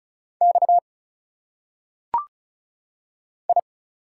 Morse code, heard as a keyed tone, about 34 wpm, 700 Hz. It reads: X I
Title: Single Letters - Phonetic Pronunciation 35wpm